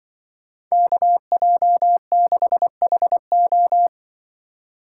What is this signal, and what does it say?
Morse code, heard as a keyed tone, about 24 wpm, 700 Hz. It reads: KJ6HO
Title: Speed Racing - USA 2x2 Call Signs 20wpm